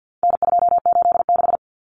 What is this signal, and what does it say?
Morse code, heard as a keyed tone, about 50 wpm, 700 Hz. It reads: N286
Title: Common Cut Numbers - 4-Digit 50wpm